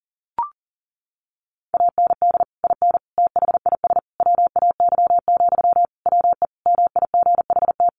Morse code, heard as a keyed tone, about 40 wpm, 700 Hz. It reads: AND IN THIS WAY, WE MIGHT
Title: Sentences from Top 1400 Words 40wpm